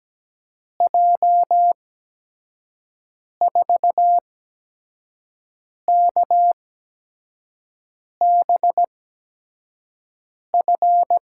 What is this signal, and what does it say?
Morse code, heard as a keyed tone, about 17 wpm, spaced as for 5 wpm, 700 Hz. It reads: J4KBF
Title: USA Call Signs - 6 Characters - 8x Character Spacing - 17wpm